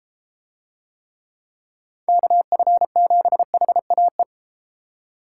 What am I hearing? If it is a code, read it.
Morse code, heard as a keyed tone, about 32 wpm, 700 Hz. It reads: KF7HAE